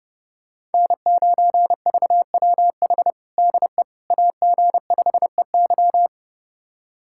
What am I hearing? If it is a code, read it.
Morse code, heard as a keyed tone, about 30 wpm, 700 Hz. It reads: N9VWH DE AG5EY
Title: USA Callsign DE Callsign - 5 Characters - 30wpm